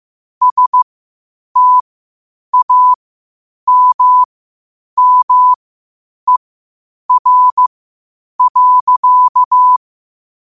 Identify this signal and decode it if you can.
Morse code, heard as a keyed tone, about 15 wpm, spaced as for 8 wpm, 1000 Hz. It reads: STAMMER.
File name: qotdfiles / 8wpm